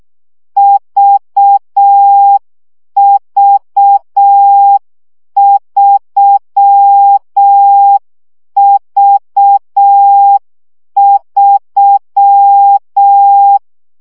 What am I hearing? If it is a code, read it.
Morse code, heard as a keyed tone, about 6 wpm, 800 Hz. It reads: VV3V3
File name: CW gyakorlas v 3 5 baud